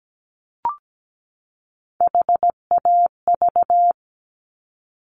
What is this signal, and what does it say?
Morse code, heard as a keyed tone, about 17 wpm, 700 Hz. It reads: HAV
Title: Top Three Letter Combinations in Words 17wpm